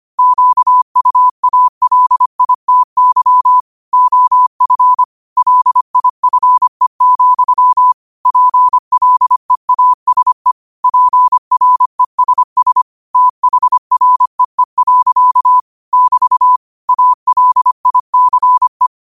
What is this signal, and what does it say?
Morse code, heard as a keyed tone, about 25 wpm, 1000 Hz. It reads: QUALITY OF LIFE, PLEASE PRESS THREE. <BT> ALICE
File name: qotdfiles / 25wpm